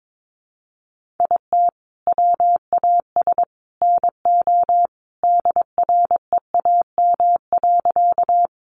Morse code, heard as a keyed tone, about 22 wpm, 700 Hz. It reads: IT WAS NO DREAM.
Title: Sentences from Top 1500 Words 22wpm